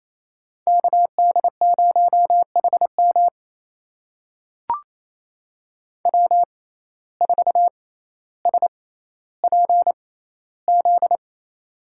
Morse code, heard as a keyed tone, about 28 wpm, spaced as for 14 wpm, 700 Hz. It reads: KD0HM W4SPZ
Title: USA Call Signs - 5 Characters - 6x Character Spacing - 28wpm